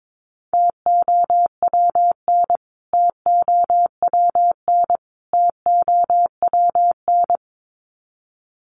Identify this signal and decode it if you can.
Morse code, heard as a keyed tone, about 22 wpm, 700 Hz. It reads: TOWN TOWN TOWN
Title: Top 500 Words Repeated 22wpm